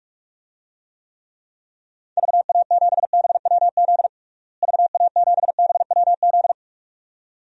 Morse code, heard as a keyed tone, about 42 wpm, 700 Hz. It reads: VA7BWZ VA7BWZ
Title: DXCC-100 Call Signs - 6 Characters - Repeated 2x - 45wpm